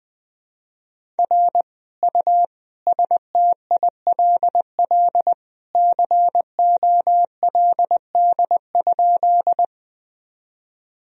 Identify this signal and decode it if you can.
Morse code, heard as a keyed tone, about 20 wpm, 700 Hz. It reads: R U STILL COLD?
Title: Sentences from Top 400 Words and Abbreviations 20wpm